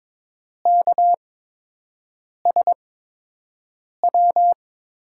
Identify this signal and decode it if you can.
Morse code, heard as a keyed tone, about 22 wpm, spaced as for 6 wpm, 700 Hz. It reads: KSW
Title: USA Call Signs - 6 Characters - 8x Character Spacing - 22wpm